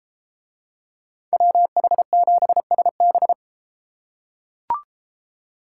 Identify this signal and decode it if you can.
Morse code, heard as a keyed tone, about 32 wpm, 700 Hz. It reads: WH7SB